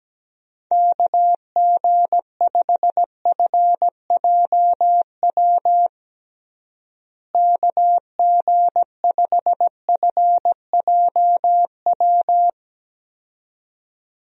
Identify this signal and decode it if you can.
Morse code, heard as a keyed tone, about 17 wpm, 700 Hz. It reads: KG5FJW KG5FJW
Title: USA Call Signs - 6 Characters - Repeated 2x - 17wpm